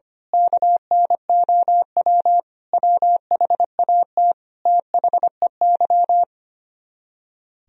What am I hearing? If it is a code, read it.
Morse code, heard as a keyed tone, about 24 wpm, 700 Hz. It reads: KNOW WHAT THEY
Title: Speed-Racing - Sets of 5 Words 25wpm